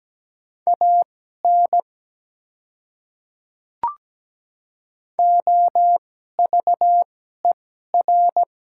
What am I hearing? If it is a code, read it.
Morse code, heard as a keyed tone, about 17 wpm, spaced as for 12 wpm, 700 Hz. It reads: AN OVER
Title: 2x Character Spacing - Top 100 Words - 17wpm